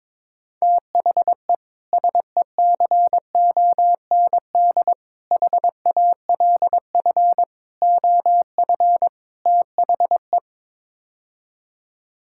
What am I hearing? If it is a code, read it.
Morse code, heard as a keyed tone, about 22 wpm, 700 Hz. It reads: THE SECOND HALF OF THE
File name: practice / 22wpm / Build A Phrase - 5-Word Phrases - 22wpm